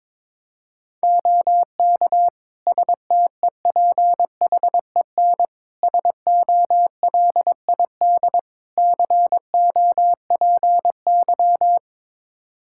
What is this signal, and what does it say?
Morse code, heard as a keyed tone, about 22 wpm, 700 Hz. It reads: OK STEPHEN SOLID COPY
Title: K1USN SST QSO Practice 22wpm